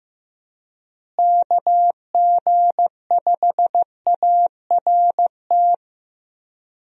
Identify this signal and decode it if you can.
Morse code, heard as a keyed tone, about 15 wpm, 700 Hz. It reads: KG5ART
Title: USA Call Signs - 6 Characters - Repeated 2x - 15wpm